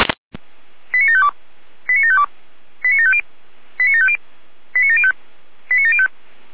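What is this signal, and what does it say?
Five-tone selective call (ZVEI), heard as quick runs of five tones, tones of about 70 ms. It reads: 89752 89752 89750 89750 89705 89705
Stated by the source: ZVEI five-tone selective call received by SDR.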